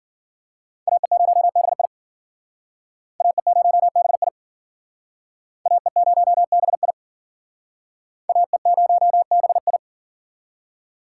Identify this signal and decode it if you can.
Morse code, heard as a keyed tone, about 54 wpm, 700 Hz. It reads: AE0BI AE0BI AE0BI AE0BI